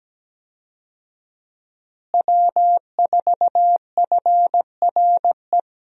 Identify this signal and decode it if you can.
Morse code, heard as a keyed tone, about 17 wpm, 700 Hz. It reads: W4FRE
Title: USA Call Signs - 5 Characters - Repeated 3x - 17wpm